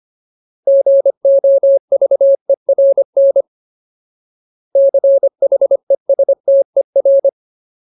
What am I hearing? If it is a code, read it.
Morse code, heard as a keyed tone, about 25 wpm, 550 Hz. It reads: GOVERN CHESTER
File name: Ultimate Word List / 25WPM Ultimate